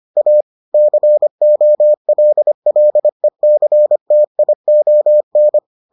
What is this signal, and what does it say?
Morse code, heard as a keyed tone, about 25 wpm, 600 Hz. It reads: A COLLECTION